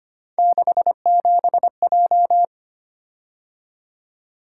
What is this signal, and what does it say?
Morse code, heard as a keyed tone, about 25 wpm, 700 Hz. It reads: 67J
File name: Three Letter-Number Combinations - 25wpm